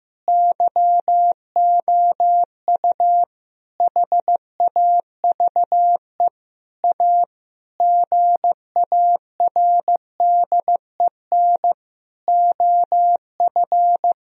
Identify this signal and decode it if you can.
Morse code, heard as a keyed tone, about 15 wpm, 700 Hz. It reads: YOU HAVE A GARDEN OF
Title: Sentences from Top 1000 Words - Review Part 4 - 15wpm